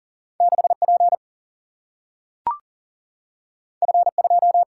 Morse code, heard as a keyed tone, about 40 wpm, 700 Hz. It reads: 6P F2
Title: Single-Letter Single-Number 40wpm